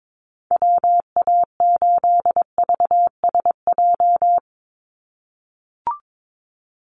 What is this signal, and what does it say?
Morse code, heard as a keyed tone, about 22 wpm, 700 Hz. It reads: WA8VSJ
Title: Call Signs - 22wpm